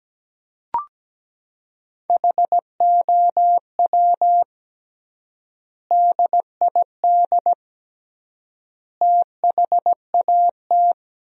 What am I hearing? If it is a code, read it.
Morse code, heard as a keyed tone, about 17 wpm, 700 Hz. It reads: HOW DID THAT
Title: 3x Word-Spacing - Sentences from Top 200 Words - 17wpm